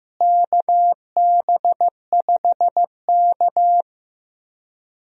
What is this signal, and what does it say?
Morse code, heard as a keyed tone, about 15 wpm, 700 Hz. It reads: KB5K
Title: USA Call Signs - 4 Characters - Repeated 3x - 15wpm